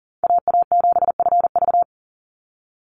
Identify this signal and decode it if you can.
Morse code, heard as a keyed tone, about 39 wpm, 700 Hz. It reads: AA7FV
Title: Call Signs - 40wpm